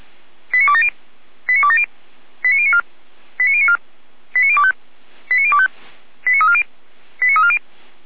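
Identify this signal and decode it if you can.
Five-tone selective call (ZVEI), heard as quick runs of five tones, tones of about 70 ms. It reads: 89270 89270 89094 89094 89025 89025 89340 89340
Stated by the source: ZVEI five-tone call received on SDR, 455 channel.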